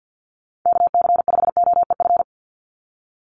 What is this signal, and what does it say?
Morse code, heard as a keyed tone, about 50 wpm, 700 Hz. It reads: KC5OEF
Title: USA Call Signs 50wpm